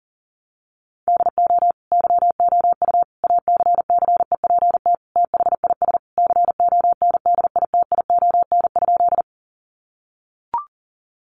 Morse code, heard as a keyed tone, about 40 wpm, 700 Hz. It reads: DO YOU ACCEPT THIS CONDITION?
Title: Sentences from Top 1300 Words 40wpm